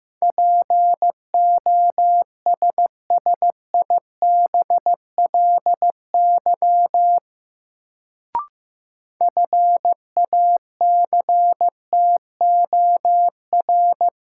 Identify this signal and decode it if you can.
Morse code, heard as a keyed tone, about 15 wpm, 700 Hz. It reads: POSSIBLY FACTOR
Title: Sentences from Top 1300 Words - New Word Drill - 15wpm